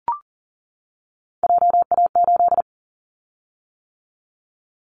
Morse code, heard as a keyed tone, about 40 wpm, 700 Hz. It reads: JA8
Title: Callsign Prefixes for Top 25 DXCC - 40wpm